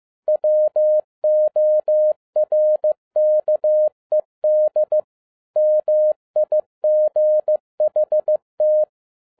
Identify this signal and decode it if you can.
Morse code, heard as a keyed tone, about 15 wpm, 610 Hz. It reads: WORKED MIGHT
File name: 2011-10-mp3 / n07-A DOT 15wpm.mp3